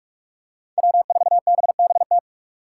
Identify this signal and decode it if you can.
Morse code, heard as a keyed tone, about 44 wpm, 700 Hz. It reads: W4BBT
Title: K1USN SST QSO Practice - 45wpm